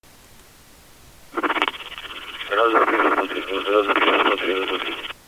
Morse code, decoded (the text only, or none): none